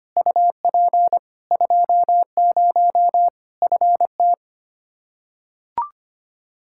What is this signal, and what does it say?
Morse code, heard as a keyed tone, about 25 wpm, 700 Hz. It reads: UP 20 FT
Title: Practice QSO - Exchange 2 - Info - 25wpm